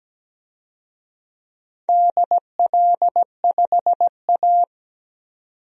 Morse code, heard as a keyed tone, about 17 wpm, 700 Hz. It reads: DL5A
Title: DXCC-100 Call Signs - 4 Characters - 17wpm